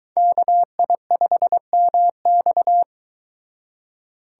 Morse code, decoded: KI5MX

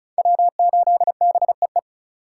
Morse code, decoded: W8BEE

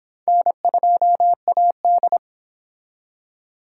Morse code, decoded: N2AD

